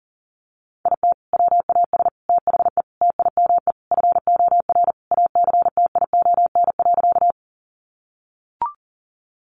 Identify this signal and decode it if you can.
Morse code, heard as a keyed tone, about 40 wpm, 700 Hz. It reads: IT WAS THE TIME FOR ACTION.
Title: Sentences from Top 700 Words 40wpm